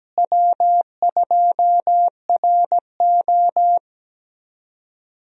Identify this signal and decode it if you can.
Morse code, heard as a keyed tone, about 17 wpm, 700 Hz. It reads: W2RO
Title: USA Call Signs - 4 Characters - Repeated 3x - 17wpm